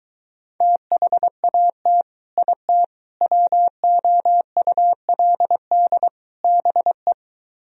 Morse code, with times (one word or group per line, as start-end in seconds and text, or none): 0.6-2.0 THAT
2.4-2.8 IT
3.2-6.1 WOULD
6.4-7.1 BE